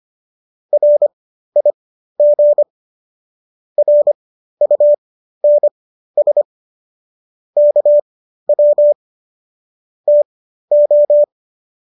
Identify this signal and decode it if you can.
Morse code, heard as a keyed tone, about 25 wpm, spaced as for 13 wpm, 590 Hz. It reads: RIG RUNS KW TO